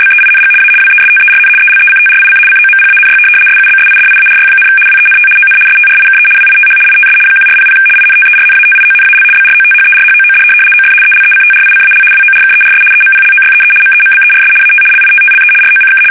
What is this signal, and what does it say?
RTTY, about 75 baud. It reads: BEING A WESTERN ELECTRIC COMPANY SUBSIDIARY, WENT ON STRIKE WITH ITS 1400 INTERNATIONAL BROTHERHOOD OF ELECTRICAL WORKERS EMPLOYEE MEMBERS AT LITTLE ROCK
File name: fsk850-75B / teletype-corp